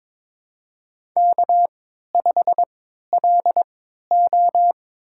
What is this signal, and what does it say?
Morse code, heard as a keyed tone, about 23 wpm, spaced as for 13 wpm, 700 Hz. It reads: K5LO
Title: USA Call Signs - 3x Character Spacing - 22wpm